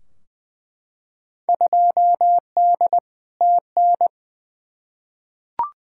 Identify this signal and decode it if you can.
Morse code, heard as a keyed tone, about 21 wpm, 700 Hz. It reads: 2D TN
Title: Field Day Exchanges Encore - 20wpm